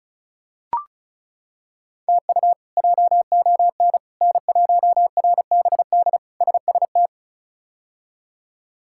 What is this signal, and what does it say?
Morse code, heard as a keyed tone, about 36 wpm, 700 Hz. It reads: TU JON N1RBD SST
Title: K1USN SST QSO Practice - 35wpm